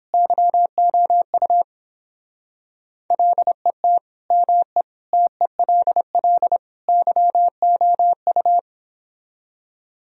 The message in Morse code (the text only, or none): YOU LET ME TELL YOU